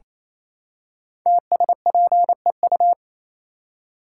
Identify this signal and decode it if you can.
Morse code, heard as a keyed tone, about 29 wpm, 700 Hz. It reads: TSPEU
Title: Code Groups - 5 Letter:Numbers - 28wpm